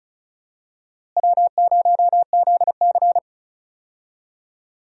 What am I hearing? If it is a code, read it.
Morse code, heard as a keyed tone, about 36 wpm, 700 Hz. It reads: W0ZC
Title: USA Call Signs - 4 Characters - Repeated 3x - 35wpm